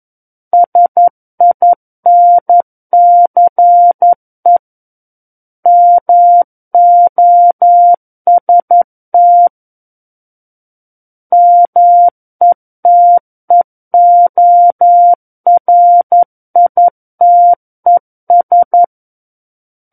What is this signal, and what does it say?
Morse code, heard as a keyed tone, about 11 wpm, 700 Hz. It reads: SINCE MOST METEORITES
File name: Book One / 02-Chapter 2 - 11 WPM